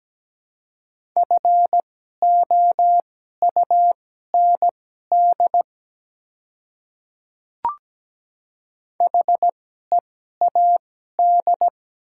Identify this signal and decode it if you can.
Morse code, heard as a keyed tone, about 17 wpm, spaced as for 12 wpm, 700 Hz. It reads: FOUND HEAD
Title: 2x Character Spacing - Top 300 Words - 17wpm